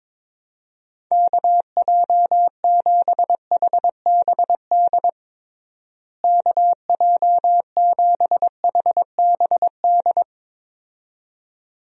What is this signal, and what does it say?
Morse code, heard as a keyed tone, about 22 wpm, 700 Hz. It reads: KJ7HBD KJ7HBD
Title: USA Call Signs - 6 Characters - Repeated 2x - 22wpm